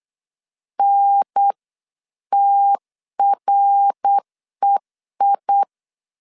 Morse code, heard as a keyed tone, about 8 wpm, 800 Hz. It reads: NTREI